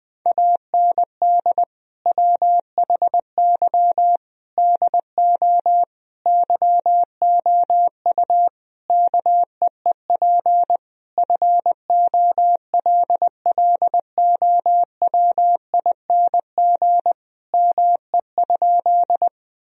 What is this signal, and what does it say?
Morse code, heard as a keyed tone, about 20 wpm, 700 Hz. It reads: AND WHY DO YOU KEEP FOLLOWING ME?
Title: Sentences from Top 1000 Words - Review Part 4 - 20wpm